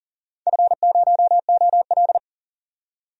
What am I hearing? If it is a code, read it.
Morse code, heard as a keyed tone, about 40 wpm, 700 Hz. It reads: F0OL
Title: DXCC-100 Call Signs - 4 Characters - 40wpm